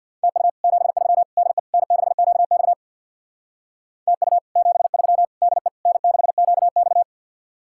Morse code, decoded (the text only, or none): TU 73 DE N6QX TU 73 DE N6QX